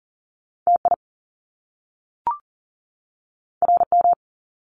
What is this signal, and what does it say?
Morse code, heard as a keyed tone, about 39 wpm, 700 Hz. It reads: TI RM
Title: Top Two Letter Combinations in Words 40wpm